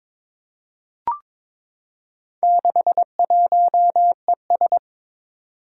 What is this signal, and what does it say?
Morse code, heard as a keyed tone, about 22 wpm, 700 Hz. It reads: 61ES
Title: Four Challenging Letter-Number Combinations 22wpm